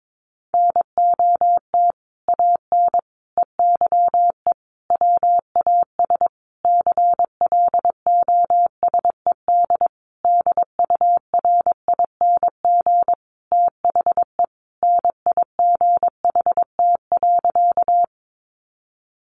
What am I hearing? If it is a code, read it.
Morse code, heard as a keyed tone, about 22 wpm, 700 Hz. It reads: NOT AN EYE WAS CLOSED DURING THE NIGHT.
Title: Sentences from Top 1400 Words 22wpm